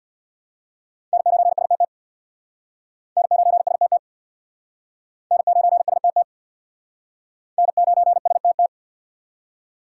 Morse code, heard as a keyed tone, about 60 wpm, 690 Hz. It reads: N9STT N9STT N9STT N9STT
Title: Speed Racing - USA 1x3 Call Signs 50wpm